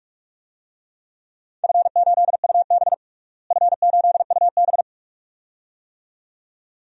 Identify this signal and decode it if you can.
Morse code, heard as a keyed tone, about 42 wpm, 700 Hz. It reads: F8UB F8UB
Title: DXCC-100 Call Signs - 4 Characters - Repeated 2x - 45wpm